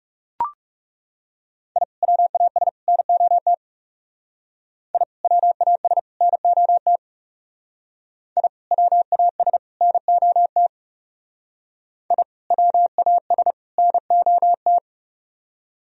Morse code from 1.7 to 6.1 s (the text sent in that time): I WAS NOT I WAS